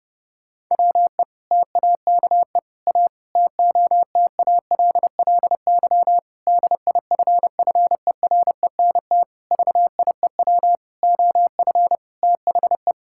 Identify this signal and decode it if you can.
Morse code, heard as a keyed tone, about 32 wpm, 700 Hz. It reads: WE TAKE A TOTALLY DIFFERENT VIEW OF THE
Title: Sentences from Top 1600 Words 30wpm